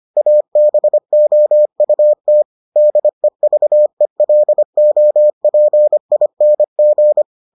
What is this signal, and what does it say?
Morse code, heard as a keyed tone, about 25 wpm, 600 Hz. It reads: ABOUT DEVELOPING